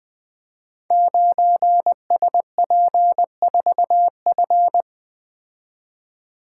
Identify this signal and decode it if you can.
Morse code, heard as a keyed tone, about 21 wpm, 700 Hz. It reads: 9SP4F
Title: Code Groups - 5 Letter:Numbers - 20wpm